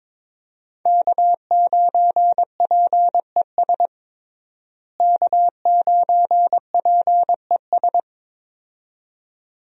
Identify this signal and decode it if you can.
Morse code, heard as a keyed tone, about 22 wpm, 700 Hz. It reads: K9PES K9PES
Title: USA Call Signs - 5 Characters - Repeated 2x - 22wpm